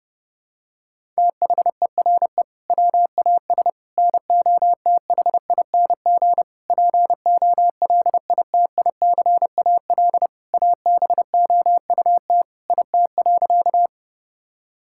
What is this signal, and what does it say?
Morse code, heard as a keyed tone, about 30 wpm, 700 Hz. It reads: THERE WAS NOTHING POLITICAL ABOUT IT.
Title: Sentences from Top 1000 Words - Review Part 5 - 30wpm